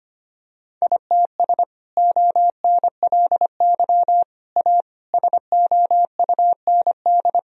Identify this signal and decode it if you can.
Morse code, heard as a keyed tone, about 25 wpm, 700 Hz. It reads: ITS ONLY A SOUND
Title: Sentences from Top 1000 Words - Review Part 1 - 25wpm